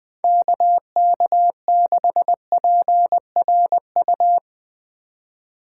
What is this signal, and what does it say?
Morse code, heard as a keyed tone, about 20 wpm, 700 Hz. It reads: KK6PRU